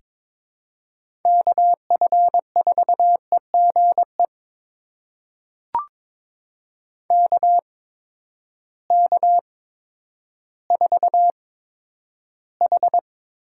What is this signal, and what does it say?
Morse code, heard as a keyed tone, about 22 wpm, spaced as for 11 wpm, 700 Hz. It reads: KF4EGE KK4H
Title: USA Call Signs - 6 Characters - 8x Character Spacing - 22wpm